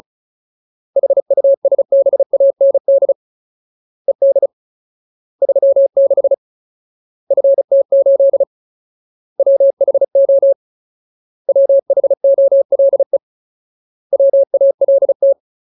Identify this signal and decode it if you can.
Morse code, heard as a keyed tone, about 34 wpm, 550 Hz. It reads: HUSBAND ED 36 FT8 WHO WHOLE WALT
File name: Ultimate Word List / 35WPM Ultimate